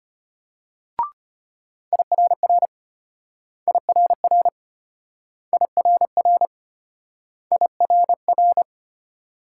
Morse code, heard as a keyed tone, about 34 wpm, 700 Hz. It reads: IRR IRR IRR IRR